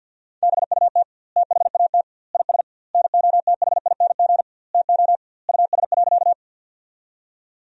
Morse code, heard as a keyed tone, about 50 wpm, 700 Hz. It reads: BUT THAT IS NOTHING TO US.
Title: 2x Word-Spacing - Sentences from Top 500 Words - 50wpm